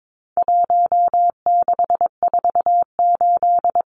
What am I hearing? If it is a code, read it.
Morse code, heard as a keyed tone, about 22 wpm, 700 Hz. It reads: 1648
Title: Numbers - 4-Digit 22wpm